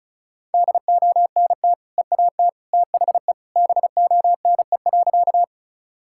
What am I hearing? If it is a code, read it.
Morse code, heard as a keyed tone, about 36 wpm, 700 Hz. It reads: DONT EAT THE BONE.
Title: Sentences from Top 1600 Words 35wpm